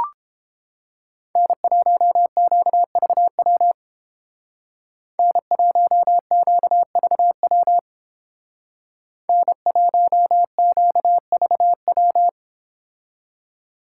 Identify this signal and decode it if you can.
Morse code, heard as a keyed tone, about 29 wpm, 700 Hz. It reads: N1QVW N1QVW N1QVW